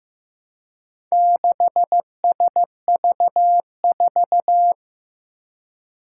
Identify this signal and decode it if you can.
Morse code, heard as a keyed tone, about 15 wpm, 700 Hz. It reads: 6SV4
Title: Four Challenging Letter-Number Combinations 15wpm